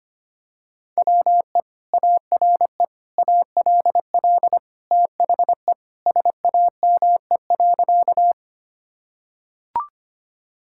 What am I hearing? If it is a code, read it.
Morse code, heard as a keyed tone, about 25 wpm, 700 Hz. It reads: WE ARE ALL THE SAME.
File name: practice / 25wpm / Sentences with 2-Word Phrases - 25wpm